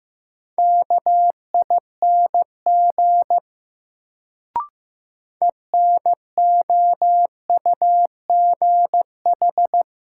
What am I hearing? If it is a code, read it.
Morse code, heard as a keyed tone, about 15 wpm, 700 Hz. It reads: KING ENOUGH
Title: Sentences from Top 400 Words - New Word Drill - 15wpm